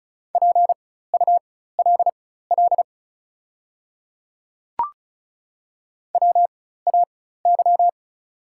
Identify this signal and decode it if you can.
Morse code, heard as a keyed tone, about 34 wpm, spaced as for 16 wpm, 690 Hz. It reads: PULL WAY
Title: 4x Character Spacing - Top 500 Words - 35wpm